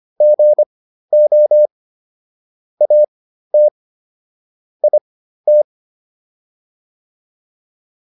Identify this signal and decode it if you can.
Morse code, heard as a keyed tone, about 25 wpm, spaced as for 13 wpm, 600 Hz. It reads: GO AT IT